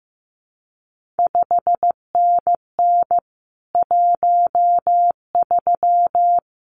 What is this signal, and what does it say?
Morse code, heard as a keyed tone, about 15 wpm, 700 Hz. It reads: 5NN 13
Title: CQ WW DX Contest Exchange Encore - 15wpm